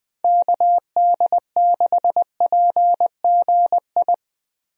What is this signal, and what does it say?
Morse code, heard as a keyed tone, about 21 wpm, 700 Hz. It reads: KD6PGI
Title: USA Call Signs - 6 Characters - Repeated 3x - 20wpm